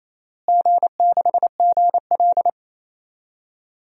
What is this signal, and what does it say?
Morse code, heard as a keyed tone, about 28 wpm, 700 Hz. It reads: G6GL
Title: DXCC-100 Call Signs - 4 Characters - Repeated 2x - 28wpm